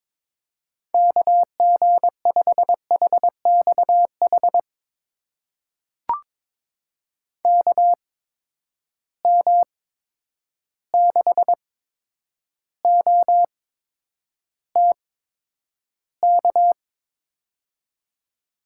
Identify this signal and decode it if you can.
Morse code, heard as a keyed tone, about 22 wpm, spaced as for 9 wpm, 700 Hz. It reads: KG5HXH KM6OTK
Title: USA Call Signs - 6 Characters - 8x Character Spacing - 22wpm